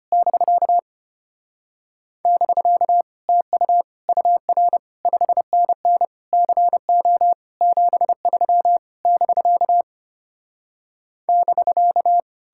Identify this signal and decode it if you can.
Morse code, heard as a keyed tone, about 33 wpm, 700 Hz. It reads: <BK> <BK> TU UR 5NN CO 73 <BK> <BK>